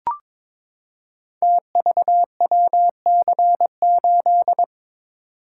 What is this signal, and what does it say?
Morse code, heard as a keyed tone, about 23 wpm, 700 Hz. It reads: TVWC8
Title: Code Groups - 5 Letter:Numbers - 22wpm